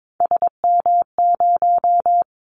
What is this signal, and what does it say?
Morse code, heard as a keyed tone, about 22 wpm, 700 Hz. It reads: SM0
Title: Callsign Prefixes for Top 25 DXCC - 22wpm